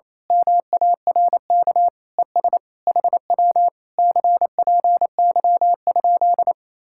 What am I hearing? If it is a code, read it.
Morse code, heard as a keyed tone, about 27 wpm, 700 Hz. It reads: MARK ES HW CPY?
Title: Speed-Racing Common QSO Phrases - 28wpm